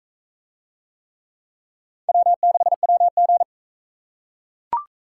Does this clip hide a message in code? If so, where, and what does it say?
Yes: Morse code from 2.1 to 3.4 s, heard as a keyed tone, about 41 wpm, 700 Hz. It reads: W6WG